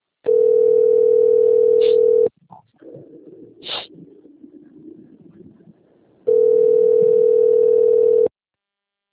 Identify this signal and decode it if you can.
Ringback tone (North America), 440 + 480 Hz.
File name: rg-1600-unknown- - - .11831